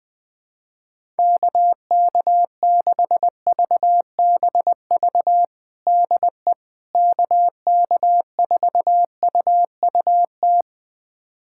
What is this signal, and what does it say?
Morse code, heard as a keyed tone, about 20 wpm, 700 Hz. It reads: KK6VBV DE KK4UUT
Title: USA Callsign DE Callsign - 6 Characters - 20wpm